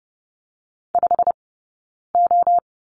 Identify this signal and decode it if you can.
Morse code, heard as a keyed tone, about 30 wpm, 700 Hz. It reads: 5 O
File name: course / 30:10wpm / 086 Lesson 15d - Number 5 - Two Characters 30wpm